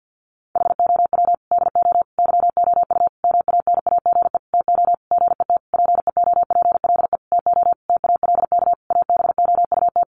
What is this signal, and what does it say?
Morse code, heard as a keyed tone, about 50 wpm, 700 Hz. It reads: HOW DO YOU MANAGE TO GET PEOPLE TO TALK ABOUT
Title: Sentences from Top 1400 Words 50wpm